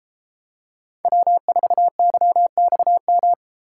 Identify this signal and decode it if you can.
Morse code, heard as a keyed tone, about 31 wpm, 700 Hz. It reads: W4YXM